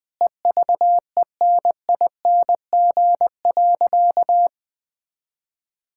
Morse code, heard as a keyed tone, about 20 wpm, 700 Hz. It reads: EVENING.
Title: Easy Sentences with Hints - 20wpm